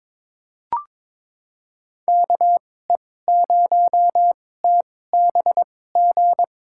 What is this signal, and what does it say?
Morse code, heard as a keyed tone, about 22 wpm, spaced as for 16 wpm, 700 Hz. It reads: KE0TBG
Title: USA Call Signs - 2x Character Spacing - 22wpm